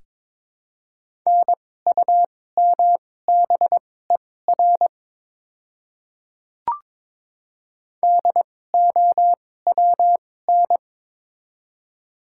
Morse code, heard as a keyed tone, about 22 wpm, spaced as for 16 wpm, 700 Hz. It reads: NUMBER DOWN
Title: 2x Character Spacing - Top 100 Words - 22wpm